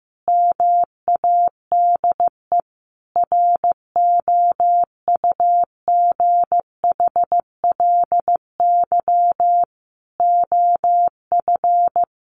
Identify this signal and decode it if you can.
Morse code, heard as a keyed tone, about 15 wpm, 700 Hz. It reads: MADE ROUGHLY OF
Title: Sentences from Top 1500 Words 15wpm